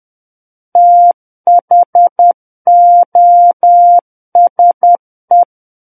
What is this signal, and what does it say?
Morse code, heard as a keyed tone, about 10 wpm, 700 Hz. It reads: THOSE
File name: Book One / 01-Chapter 1 - 10 WPM